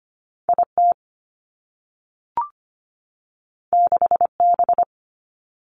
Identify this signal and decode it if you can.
Morse code, heard as a keyed tone, about 25 wpm, 700 Hz. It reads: IT 6B
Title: Two Challenging Letter-Number Combinations 25wpm